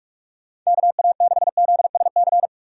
Morse code, heard as a keyed tone, about 44 wpm, 700 Hz. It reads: KA6ZSC
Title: USA Call Signs - 6 Characters - 6x Character Spacing - 45wpm